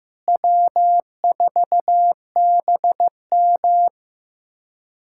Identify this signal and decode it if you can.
Morse code, heard as a keyed tone, about 16 wpm, 700 Hz. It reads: W4BM
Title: USA Callsign DE Callsign - 4 Characters - 15wpm